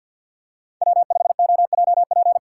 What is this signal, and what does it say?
Morse code, heard as a keyed tone, about 53 wpm, 700 Hz. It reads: W5OJP
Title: USA Call Signs - 5 Characters - 6x Character Spacing - 50wpm